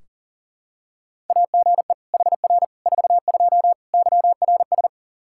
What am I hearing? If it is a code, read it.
Morse code, heard as a keyed tone, about 40 wpm, 700 Hz. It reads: AGE HR 42 YRS
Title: Practice QSO - Exchange 3 - More Info - 40wpm